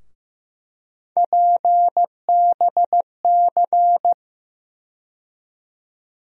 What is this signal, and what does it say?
Morse code, heard as a keyed tone, about 15 wpm, 700 Hz. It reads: PBC
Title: USA Call Sign Postfixes - 3 Letters 15wpm